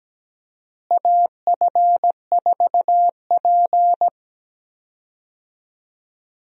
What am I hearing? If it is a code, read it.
Morse code, heard as a keyed tone, about 17 wpm, 700 Hz. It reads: AF4P
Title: USA Call Signs - 4 Characters - Repeated 3x - 17wpm